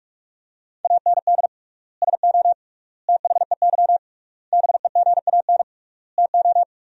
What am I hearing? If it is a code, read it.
Morse code, heard as a keyed tone, about 46 wpm, 700 Hz. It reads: AND SO THEY BEGAN TO
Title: 3x Word-Spacing - Sentences from Top 400 Words - 45wpm